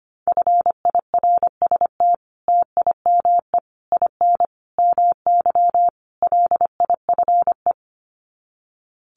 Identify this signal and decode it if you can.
Morse code, heard as a keyed tone, about 25 wpm, 700 Hz. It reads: FIRST TIME IN MY LIFE
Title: Sets of 5 Words Encore2 25wpm